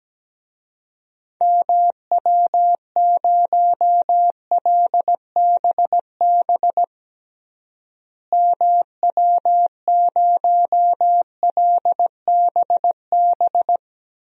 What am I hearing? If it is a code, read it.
Morse code, heard as a keyed tone, about 17 wpm, 700 Hz. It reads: MW0LBB MW0LBB
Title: DXCC-100 Call Signs - 6 Characters - Repeated 2x - 17wpm